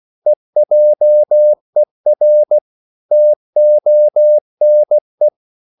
Morse code, heard as a keyed tone, about 16 wpm, 600 Hz. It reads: EJER TONE
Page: Korte danske ord 16wpm